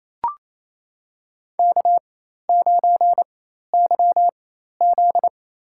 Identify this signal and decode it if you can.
Morse code, heard as a keyed tone, about 28 wpm, spaced as for 13 wpm, 700 Hz. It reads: K9YZ
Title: USA Call Signs - 4 Characters - 4x Character Spacing - 28wpm